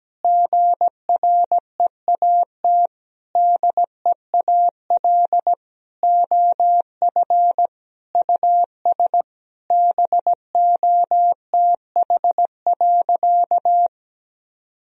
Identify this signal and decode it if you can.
Morse code, heard as a keyed tone, about 17 wpm, 700 Hz. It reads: GREAT DEAL OF US BOTH.
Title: Sentences from Top 1000 Words - Review Part 4 - 17wpm